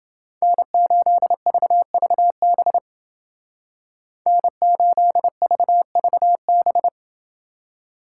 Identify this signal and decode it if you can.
Morse code, heard as a keyed tone, about 29 wpm, 700 Hz. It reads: N8VVB N8VVB